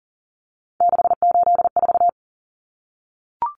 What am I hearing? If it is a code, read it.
Morse code, heard as a keyed tone, about 40 wpm, 700 Hz. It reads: 684
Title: Numbers - 3-Digit 40wpm